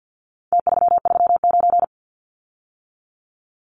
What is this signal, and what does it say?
Morse code, heard as a keyed tone, about 50 wpm, 700 Hz. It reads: T339
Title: Numbers - Contest Style - 4-Digit 50wpm